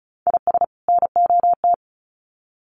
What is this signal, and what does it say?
Morse code, heard as a keyed tone, about 34 wpm, 700 Hz. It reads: IS NOT
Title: Sets of 2 Words 35wpm